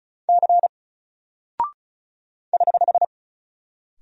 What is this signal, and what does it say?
Morse code, heard as a keyed tone, about 36 wpm, 700 Hz. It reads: C <HH>
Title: Single Character (Letter:Number:Punctuation:Prosign) 35wpm